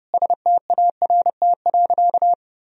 Morse code, heard as a keyed tone, about 30 wpm, 700 Hz. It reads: START.
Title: Sentences from Top 1000 Words - Review Part 1 - 30wpm